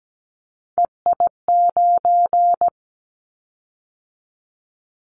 Morse code, heard as a keyed tone, about 17 wpm, 700 Hz. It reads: EI9
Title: Callsign Prefixes for Top 50 DXCC - 17wpm